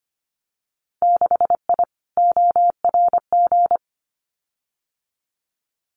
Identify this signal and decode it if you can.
Morse code, heard as a keyed tone, about 25 wpm, 700 Hz. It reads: 6I ORG
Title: Winter Field Day Exchanges Encore 25wpm